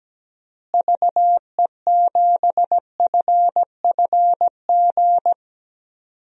Morse code, heard as a keyed tone, about 17 wpm, 700 Hz. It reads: VE7FFG
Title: DXCC-100 Call Signs - 6 Characters - Repeated 2x - 17wpm